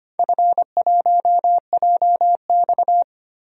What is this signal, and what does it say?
Morse code, heard as a keyed tone, about 25 wpm, 700 Hz. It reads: F1JX
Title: DXCC-100 Call Signs - 4 Characters - 25wpm